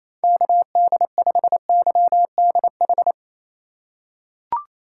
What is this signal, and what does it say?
Morse code, heard as a keyed tone, about 28 wpm, 700 Hz. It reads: KD5YDH
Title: USA Call Signs - 6 Characters - 2x Character Spacing - 28wpm